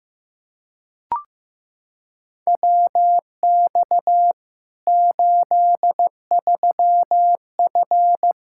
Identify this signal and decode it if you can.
Morse code, heard as a keyed tone, about 15 wpm, 700 Hz. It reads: WX 83F
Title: Practice QSO - Exchange 2 - Info - 15wpm